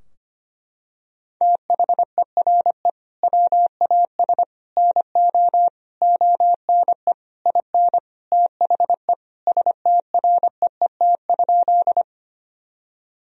Morse code, heard as a keyed tone, about 25 wpm, 700 Hz. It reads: THERE WAS NO ONE IN THE STREET?
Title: Sentences from Top 1000 Words - Review Part 3 - 25wpm